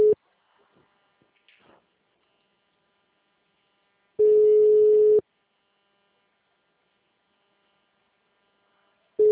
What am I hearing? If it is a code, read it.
Ringback tone (Europe, 425 Hz).